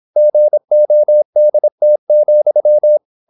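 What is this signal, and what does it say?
Morse code, heard as a keyed tone, about 26 wpm, 600 Hz. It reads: GODT,
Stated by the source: High-speed (QRQ) Morse at 26 wpm.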